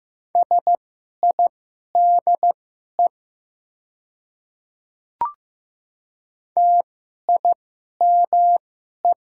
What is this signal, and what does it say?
Morse code, heard as a keyed tone, about 15 wpm, spaced as for 11 wpm, 700 Hz. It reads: SIDE TIME
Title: 2x Character Spacing - Top 200 Words - 15wpm